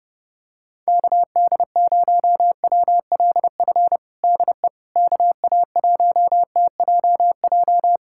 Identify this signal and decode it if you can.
Morse code, heard as a keyed tone, about 30 wpm, 700 Hz. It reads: KD0WLF DE KA1TJJ
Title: USA Callsign DE Callsign - 6 Characters - 30wpm